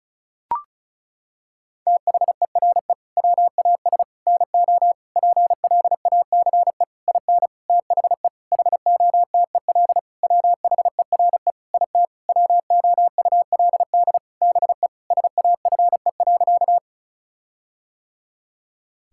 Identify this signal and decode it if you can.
Morse code, heard as a keyed tone, about 34 wpm, 700 Hz. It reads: THERE WAS NO PLACE IN THE HOTEL WHERE IT WOULD BE SAFE.
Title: Sentences from Top 1600 Words 35wpm